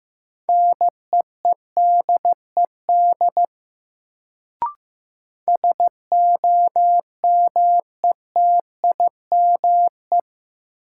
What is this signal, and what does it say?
Morse code, heard as a keyed tone, about 15 wpm, 700 Hz. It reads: NEEDED SOMETIME
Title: Sentences from Top 1000 Words - Review Part 6 - New Word Drill - 15wpm